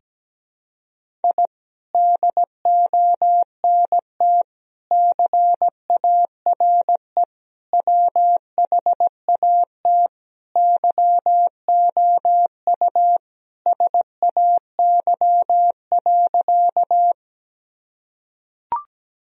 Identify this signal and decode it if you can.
Morse code, heard as a keyed tone, about 17 wpm, 700 Hz. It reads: I DONT CARE WHAT YOU SAY.
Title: Sentences from Top 1000 Words - Review Part 1 - 17wpm